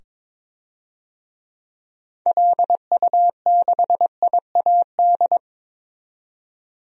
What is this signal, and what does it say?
Morse code, heard as a keyed tone, about 22 wpm, 700 Hz. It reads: LU6IAD
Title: DXCC-100 Call Signs - 6 Characters - 22wpm